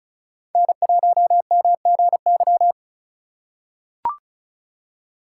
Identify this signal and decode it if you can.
Morse code, heard as a keyed tone, about 36 wpm, 690 Hz. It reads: N1MGY
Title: USA Call Signs - 3x Character Spacing - 35wpm